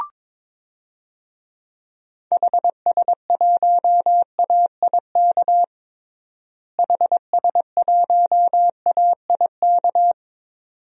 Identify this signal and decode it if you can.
Morse code, heard as a keyed tone, about 22 wpm, 700 Hz. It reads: HS1AIK HS1AIK
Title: DXCC-100 Call Signs - 6 Characters - Repeated 2x - 22wpm